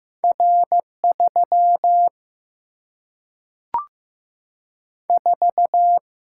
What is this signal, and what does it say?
Morse code, heard as a keyed tone, about 15 wpm, 700 Hz. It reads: R 3 4
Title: Single-Letter Single-Number 15wpm